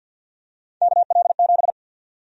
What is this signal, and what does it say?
Morse code, heard as a keyed tone, about 52 wpm, 700 Hz. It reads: KL7
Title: USA Call Sign Prefixes - Repeated 2x - 50wpm